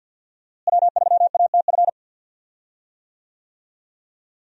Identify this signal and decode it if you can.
Morse code, heard as a keyed tone, about 46 wpm, 700 Hz. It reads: W3ATF
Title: USA Call Signs - 5 Characters - Repeated 2x - 50wpm